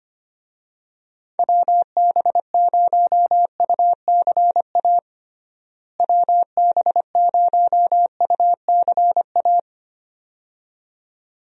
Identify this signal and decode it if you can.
Morse code, heard as a keyed tone, about 25 wpm, 700 Hz. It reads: WB0UCA WB0UCA
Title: USA Call Signs - 6 Characters - Repeated 2x - 25wpm